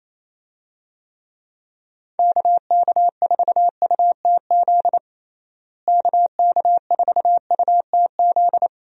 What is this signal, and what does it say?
Morse code, heard as a keyed tone, about 27 wpm, 700 Hz. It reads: KK4UTZ KK4UTZ
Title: USA Call Signs - 6 Characters - Repeated 2x - 28wpm